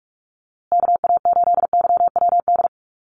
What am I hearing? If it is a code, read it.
Morse code, heard as a keyed tone, about 44 wpm, 700 Hz. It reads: KA8YWD
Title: USA Call Signs Repeated 2x 45wpm